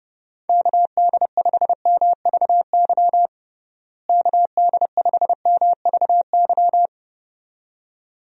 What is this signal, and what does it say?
Morse code, heard as a keyed tone, about 30 wpm, 700 Hz. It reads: KD5MVY KD5MVY
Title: USA Call Signs - 6 Characters - Repeated 2x - 30wpm